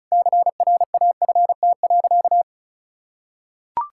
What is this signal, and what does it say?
Morse code, heard as a keyed tone, about 36 wpm, 700 Hz. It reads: CRAFT.
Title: Sentences from Top 1000 Words - Review Part 6 - 35wpm